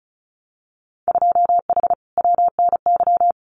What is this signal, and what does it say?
Morse code, heard as a keyed tone, about 34 wpm, 700 Hz. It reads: 2H WNY
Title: Winter Field Day Exchanges Encore 35wpm